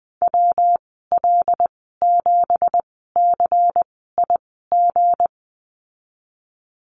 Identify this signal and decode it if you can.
Morse code, heard as a keyed tone, about 20 wpm, spaced as for 15 wpm, 700 Hz. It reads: WL7CIG
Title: 173 Lesson 28g - Number 9 - Call Signs 20wpm